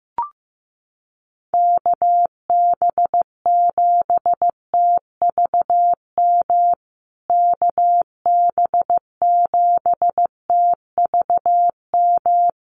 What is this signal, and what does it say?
Morse code, heard as a keyed tone, about 15 wpm, 700 Hz. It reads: KB7TVM KB7TVM
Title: USA Call Signs Repeated 3x 15wpm